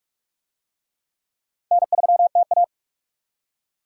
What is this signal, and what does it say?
Morse code, heard as a keyed tone, about 44 wpm, 700 Hz. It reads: N3TA